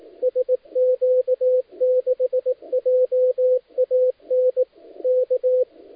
Morse code, heard as a keyed tone, about 18 wpm, 510 Hz, with radio noise. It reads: SQ6JAN K
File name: dlazaawansowanych / nagraniazpasma / titawaNETcontrol